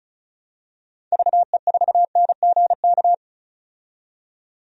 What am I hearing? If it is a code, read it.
Morse code, heard as a keyed tone, about 34 wpm, 700 Hz. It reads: VE4NGK
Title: DXCC-100 Call Signs - 6 Characters - 35wpm